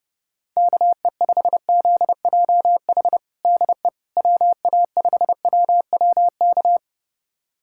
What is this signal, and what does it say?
Morse code, heard as a keyed tone, about 30 wpm, 700 Hz. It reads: KE5ZJH DE WA5WWK
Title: USA Callsign DE Callsign - 6 Characters - 30wpm